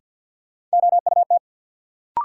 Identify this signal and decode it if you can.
Morse code, heard as a keyed tone, about 49 wpm, 700 Hz. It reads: OUT
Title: Sentences from Top 100 Words - New Word Drill - 50wpm